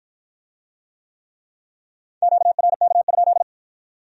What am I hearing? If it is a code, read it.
Morse code, heard as a keyed tone, about 53 wpm, 700 Hz. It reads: QRK?